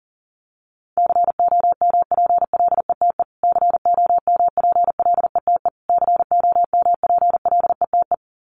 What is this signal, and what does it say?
Morse code, heard as a keyed tone, about 40 wpm, 700 Hz. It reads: COMPLETE COMPLETE COMPLETE
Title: Top 500 Words Repeated 40wpm